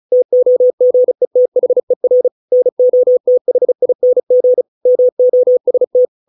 Morse code, heard as a keyed tone, about 34 wpm, 500 Hz. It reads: TOGETHER NOTHING MOST